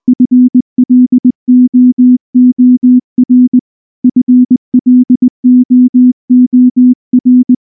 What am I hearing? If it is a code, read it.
Morse code, heard as a keyed tone, about 20 wpm, 250 Hz. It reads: FLOOR FLOOR
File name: morse / morse-training-20WPM-9